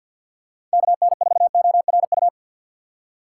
Morse code, heard as a keyed tone, about 52 wpm, 700 Hz. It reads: KN4ORU
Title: USA Call Signs - 6 Characters - 3x Character Spacing - 50wpm